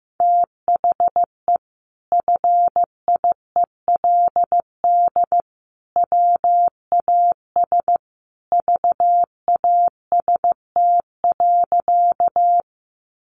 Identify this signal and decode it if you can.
Morse code, heard as a keyed tone, about 15 wpm, 700 Hz. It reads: THE FIELD WAS VAST.
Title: Sentences from Top 1300 Words 15wpm